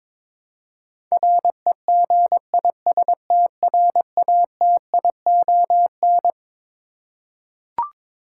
Morse code, heard as a keyed tone, about 22 wpm, 700 Hz. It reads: REGISTRATION
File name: practice / 22wpm / Twelve Letter Words - 22wpm